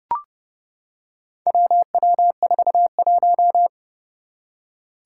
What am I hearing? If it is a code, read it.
Morse code, heard as a keyed tone, about 30 wpm, 700 Hz. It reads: WW41
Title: Four Challenging Letter-Number Combinations 30wpm